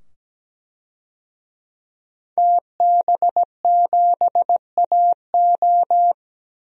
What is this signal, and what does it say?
Morse code, heard as a keyed tone, about 17 wpm, 700 Hz. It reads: TB7AO
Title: DXCC-100 Call Signs - 5 Characters - 17wpm